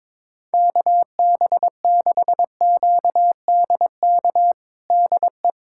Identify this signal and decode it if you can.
Morse code, heard as a keyed tone, about 22 wpm, 700 Hz. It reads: KB6QDK DE
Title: USA Callsign DE Callsign 22wpm